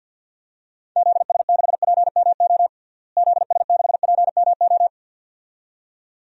Morse code, heard as a keyed tone, about 48 wpm, 700 Hz. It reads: ZS6PMO ZS6PMO
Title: DXCC-100 Call Signs - 6 Characters - Repeated 2x - 50wpm